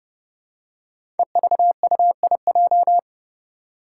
Morse code, heard as a keyed tone, about 30 wpm, 700 Hz. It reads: EVUIJ
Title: Five Challenging Letter-Number Combinations 30wpm